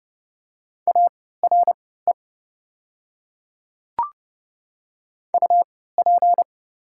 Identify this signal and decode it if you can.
Morse code, heard as a keyed tone, about 30 wpm, spaced as for 17 wpm, 700 Hz. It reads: ARE UP
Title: 3x Character Spacing - Top 100 Words - 30wpm